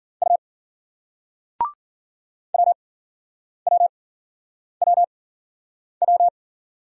Morse code, heard as a keyed tone, about 50 wpm, spaced as for 9 wpm, 700 Hz. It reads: U WWWW
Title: Speed-Racing Single Letters 40wpm